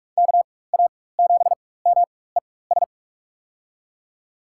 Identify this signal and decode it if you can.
Morse code, heard as a keyed tone, about 41 wpm, spaced as for 20 wpm, 700 Hz. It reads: KA7MES
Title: USA Call Signs - 6 Characters - 4x Character Spacing - 45wpm